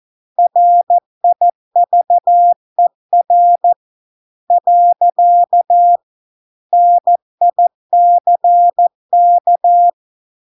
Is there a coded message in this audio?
Yes: Morse code from 0.4 to 9.9 s, heard as a keyed tone, about 14 wpm, 700 Hz. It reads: RIVER . NICK